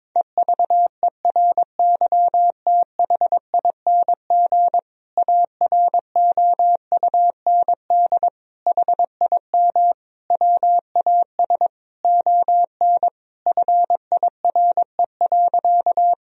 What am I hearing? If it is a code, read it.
Morse code, heard as a keyed tone, about 22 wpm, 700 Hz. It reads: EVERYTHING AROUND HIM WAS ON FIRE.
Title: Sentences from Top 1000 Words - Review Part 3 - 22wpm